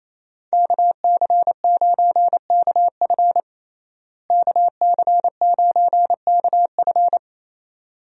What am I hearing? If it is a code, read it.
Morse code, heard as a keyed tone, about 28 wpm, 700 Hz. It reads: KC9KF KC9KF
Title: USA Call Signs - 5 Characters - Repeated 2x - 28wpm